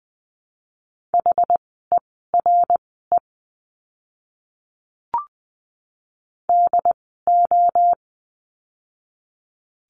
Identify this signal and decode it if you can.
Morse code, heard as a keyed tone, about 20 wpm, spaced as for 14 wpm, 700 Hz. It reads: HERE DO
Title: 075 Lesson 13f - Letter L - Words 20wpm